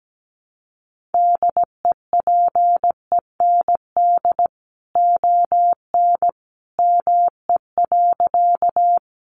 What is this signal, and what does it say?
Morse code, heard as a keyed tone, about 17 wpm, 700 Hz. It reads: DEPEND ON ME.
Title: Sentences from Top 1300 Words 17wpm